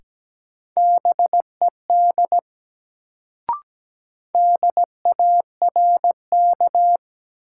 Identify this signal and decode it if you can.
Morse code, heard as a keyed tone, about 17 wpm, 700 Hz. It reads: BED DARK
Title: Sentences from Top 500 Words - New Word Drill - 17wpm